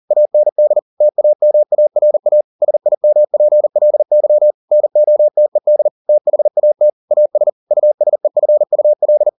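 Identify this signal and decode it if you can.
Morse code, heard as a keyed tone, about 40 wpm, 600 Hz. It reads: AND TAMARA SIMPLY NOTED THAT AS USEFUL
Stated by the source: Morse code at 40 wpm, practice text, CW med Gnister.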